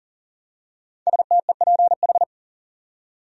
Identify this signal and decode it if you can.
Morse code, heard as a keyed tone, about 39 wpm, 700 Hz. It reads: STEPH
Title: USA Common Names - 40wpm